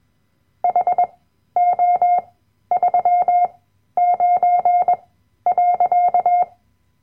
Morse code, heard as a keyed tone, about 21 wpm, spaced as for 12 wpm, 710 Hz. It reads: HO39.
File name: Workshop Dateien / CW-Online-Kurs / Lektion 010